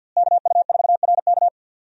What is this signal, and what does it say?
Morse code, heard as a keyed tone, about 45 wpm, 700 Hz. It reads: KU4RK
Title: USA Call Signs - 5 Characters - Repeated 2x - 50wpm